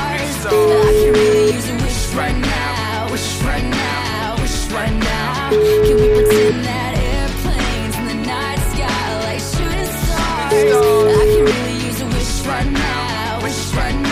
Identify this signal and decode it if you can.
Ringback tone (Europe, 425 Hz), with noise.